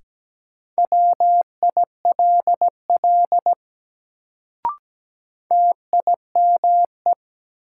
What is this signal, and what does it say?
Morse code, heard as a keyed tone, about 17 wpm, 700 Hz. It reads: WILL TIME
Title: Sets of 2 Words - New Word Drill - 17wpm